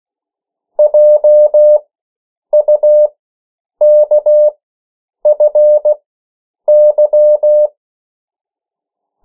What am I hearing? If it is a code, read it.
Morse code, heard as a keyed tone, about 16 wpm, spaced as for 8 wpm, 600 Hz. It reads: JUKFY
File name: morse mp3 / 14 urok 13